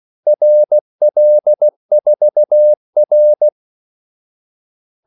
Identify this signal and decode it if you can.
Morse code, heard as a keyed tone, about 16 wpm, 600 Hz. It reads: RL4R
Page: Kaldesignaler 16 wpm